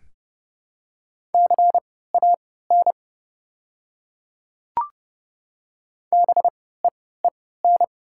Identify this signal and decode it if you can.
Morse code, heard as a keyed tone, about 30 wpm, spaced as for 17 wpm, 700 Hz. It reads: CAN BEEN
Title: 3x Character Spacing - Top 100 Words - 30wpm